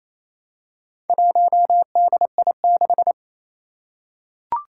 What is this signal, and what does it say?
Morse code, heard as a keyed tone, about 28 wpm, 700 Hz. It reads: 1DI6
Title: Four Challenging Letter-Number Combinations 28wpm